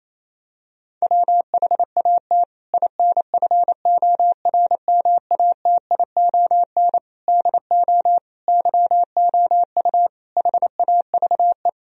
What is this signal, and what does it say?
Morse code, heard as a keyed tone, about 28 wpm, 700 Hz. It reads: WHAT INFORMATION DO YOU HAVE
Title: Sentences from Top 1000 Words - Review Part 3 - 28wpm